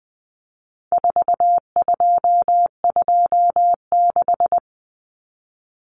Morse code, heard as a keyed tone, about 20 wpm, 700 Hz. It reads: 4226
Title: Common Cut Numbers - 4-Digit 20wpm